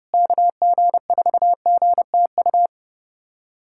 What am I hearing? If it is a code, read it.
Morse code, heard as a keyed tone, about 30 wpm, 700 Hz. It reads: KG4GTU
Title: USA Callsign DE Callsign - 6 Characters - 30wpm